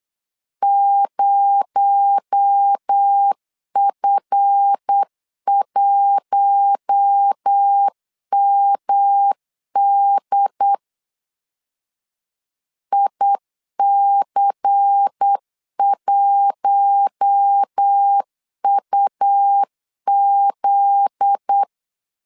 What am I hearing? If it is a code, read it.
Morse code, heard as a keyed tone, about 8 wpm, 800 Hz. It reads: 0F1MD IC1UZ
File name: LEZIONI / 7 GF9Z / 7 GF9Z 8wpm 3213